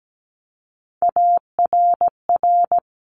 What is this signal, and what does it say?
Morse code, heard as a keyed tone, about 17 wpm, 700 Hz. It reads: ARR
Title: Top Three Letter Combinations in Words 17wpm